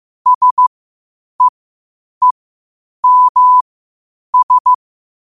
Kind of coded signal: Morse code